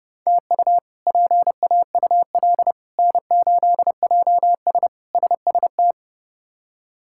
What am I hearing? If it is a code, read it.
Morse code, heard as a keyed tone, about 30 wpm, 700 Hz. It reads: TU PAUL N8JS SST
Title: K1USN SST QSO Practice - 30wpm